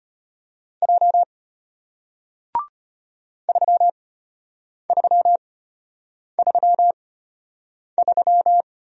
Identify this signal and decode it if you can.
Morse code, heard as a keyed tone, about 33 wpm, spaced as for 8 wpm, 700 Hz. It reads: J 3333